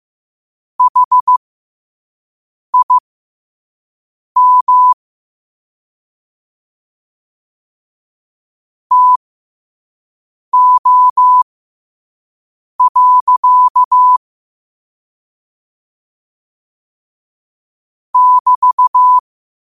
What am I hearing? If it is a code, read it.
Morse code, heard as a keyed tone, about 15 wpm, spaced as for 5 wpm, 1000 Hz. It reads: HIM TO. <BT>
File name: qotdfiles / 5wpm